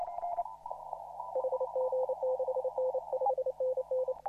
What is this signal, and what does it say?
Morse code, heard as a keyed tone, about 29 wpm, 520 Hz, with noise. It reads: HG6N 5NN
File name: CW / RHOB CW 2017 HA6P 0001